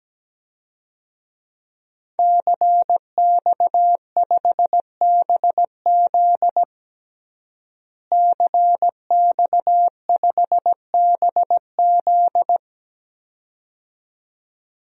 Morse code, heard as a keyed tone, about 17 wpm, 700 Hz. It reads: CX5BZ CX5BZ
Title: DXCC-100 Call Signs - 5 Characters - Repeated 2x - 17wpm